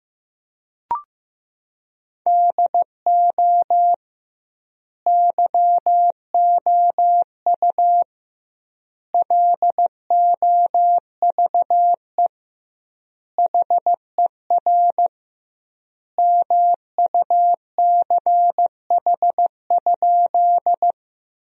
Morse code, heard as a keyed tone, about 15 wpm, 700 Hz. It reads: DO YOU LOVE HER MUCH?
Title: 2x Word-Spacing - Sentences from Top 400 Words - 15wpm